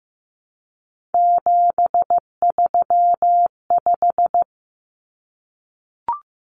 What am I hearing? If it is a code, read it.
Morse code, heard as a keyed tone, about 15 wpm, 700 Hz. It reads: 735
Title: Common Cut Numbers - 3-Digit 15wpm